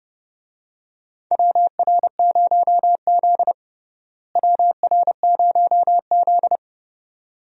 Morse code, heard as a keyed tone, about 30 wpm, 700 Hz. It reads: WR0Z WR0Z
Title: USA Call Signs - 4 Characters - Repeated 2x - 30wpm